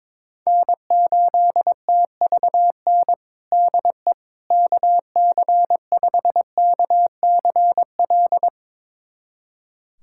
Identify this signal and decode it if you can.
Morse code, heard as a keyed tone, about 22 wpm, 700 Hz. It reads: N8TVN DE KC5KCL
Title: USA Callsign DE Callsign 22wpm